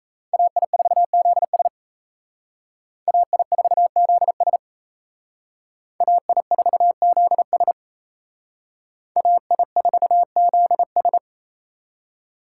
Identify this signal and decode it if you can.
Morse code, heard as a keyed tone, about 34 wpm, 700 Hz. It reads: AI4ZS AI4ZS AI4ZS AI4ZS
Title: Speed Racing - USA 2x2 Call Signs 28wpm